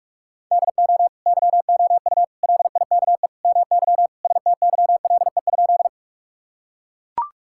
Morse code, heard as a keyed tone, about 41 wpm, 700 Hz. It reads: DO YOU LIKE MY STYLE?
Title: Sentences from Top 1000 Words - Review Part 5 - 45wpm